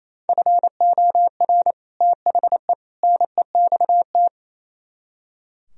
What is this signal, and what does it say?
Morse code, heard as a keyed tone, about 36 wpm, 700 Hz. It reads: FOR THE NEXT